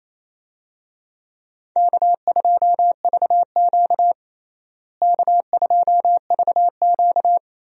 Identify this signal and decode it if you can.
Morse code, heard as a keyed tone, about 28 wpm, 700 Hz. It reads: K2VQ K2VQ
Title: USA Call Signs - 4 Characters - Repeated 2x - 28wpm